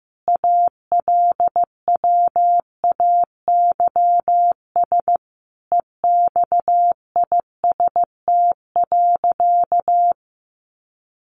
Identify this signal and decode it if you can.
Morse code, heard as a keyed tone, about 14 wpm, 700 Hz. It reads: ALWAYS EXIST.
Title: Sentences from Top 1500 Words 15wpm